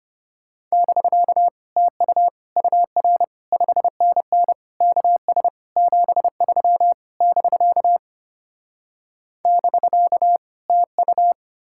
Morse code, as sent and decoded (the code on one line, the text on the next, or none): -...-.- / - ..- / ..- .-. / ..... -. -. / -.- ... / --... ...-- / -...-.- / -...-.- / - ..-
<BK> TU UR 5NN KS 73 <BK> <BK> TU